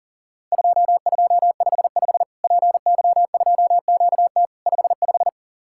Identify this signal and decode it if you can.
Morse code, heard as a keyed tone, about 40 wpm, 700 Hz. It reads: 2255 PY2QT 55
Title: Worked All Europe DX Contest - QTC Exchange - 40wpm